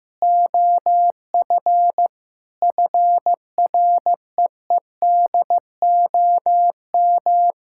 Morse code, heard as a keyed tone, about 15 wpm, 700 Hz. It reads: OF FREEDOM
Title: Sentences from Top 1000 Words - Review Part 5 - 15wpm